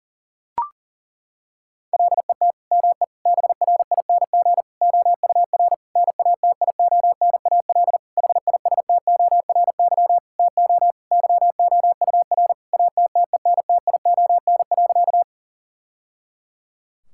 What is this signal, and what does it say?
Morse code, heard as a keyed tone, about 40 wpm, 700 Hz. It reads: LET ME BRING OUR NATIONAL HISTORY TO YOUR ATTENTION.
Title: Sentences from Top 1000 Words - Review Part 6 - 40wpm